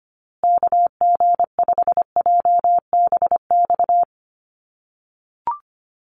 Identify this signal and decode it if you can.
Morse code, heard as a keyed tone, about 25 wpm, 700 Hz. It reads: KG5JBX
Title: USA Call Signs Repeated 3x 25wpm